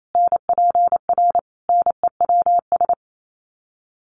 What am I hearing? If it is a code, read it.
Morse code, heard as a keyed tone, about 28 wpm, 700 Hz. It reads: NPR NEWS
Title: Sets of 5 Words Encore 28wpm